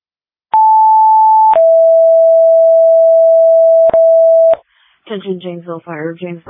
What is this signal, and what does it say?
Two-tone page, heard as a one-second tone, then a three-second tone, about 880 then 660 Hz, with speech. It is heard from 0.5 to 4.5 s.